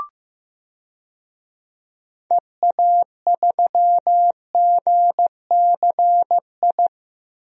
Morse code, heard as a keyed tone, about 15 wpm, 700 Hz. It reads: EA3GCI
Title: DXCC-100 Call Signs - 6 Characters - Repeated 2x - 15wpm